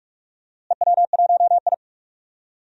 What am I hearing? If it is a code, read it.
Morse code, heard as a keyed tone, about 46 wpm, 700 Hz. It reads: EW1I
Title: DXCC-100 Call Signs - 4 Characters - Repeated 2x - 45wpm